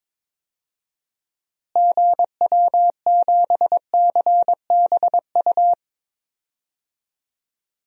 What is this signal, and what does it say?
Morse code, heard as a keyed tone, about 22 wpm, 700 Hz. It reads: GW7CBU
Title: DXCC-100 Call Signs - 6 Characters - Repeated 2x - 22wpm